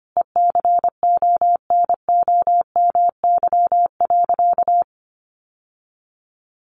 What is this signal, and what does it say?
Morse code, heard as a keyed tone, about 25 wpm, 700 Hz. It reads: ECONOMY.
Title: Sentences from Top 1200 Words 25wpm